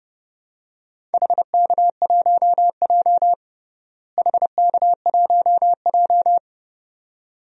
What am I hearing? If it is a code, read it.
Morse code, heard as a keyed tone, about 31 wpm, 700 Hz. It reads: HK1J HK1J
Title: DXCC-100 Call Signs - 4 Characters - Repeated 2x - 30wpm